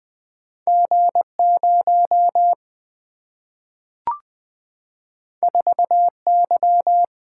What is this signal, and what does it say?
Morse code, heard as a keyed tone, about 20 wpm, 700 Hz. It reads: G 0 4 Y
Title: Single-Letter Single-Number 20wpm